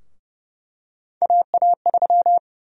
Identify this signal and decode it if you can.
Morse code, heard as a keyed tone, about 30 wpm, 700 Hz. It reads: AA3
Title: USA Call Sign Prefixes - Repeated 2x - 30wpm